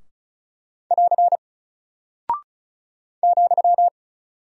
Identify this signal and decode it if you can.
Morse code, heard as a keyed tone, about 36 wpm, 700 Hz. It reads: <AR> ,
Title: Single Character (Letter:Number:Punctuation:Prosign) Alt 35wpm